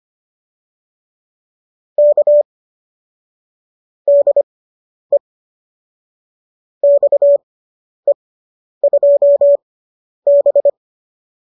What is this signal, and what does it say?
Morse code, heard as a keyed tone, about 25 wpm, spaced as for 10 wpm, 570 Hz. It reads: K DE XE2B